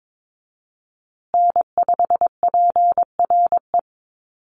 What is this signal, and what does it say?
Morse code, heard as a keyed tone, about 22 wpm, 700 Hz. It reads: N5PRE
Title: USA Call Signs Repeated 2x 22wpm